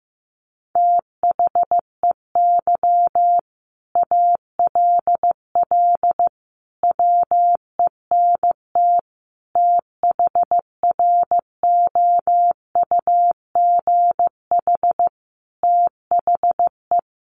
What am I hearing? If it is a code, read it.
Morse code, heard as a keyed tone, about 15 wpm, 700 Hz. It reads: THEY ALL WENT THROUGH THE
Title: Sentences from Top 1300 Words 15wpm